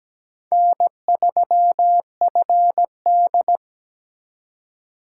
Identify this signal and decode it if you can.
Morse code, heard as a keyed tone, about 20 wpm, 700 Hz. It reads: N3FD
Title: Speed Racing - USA 1x2 Call Signs 17wpm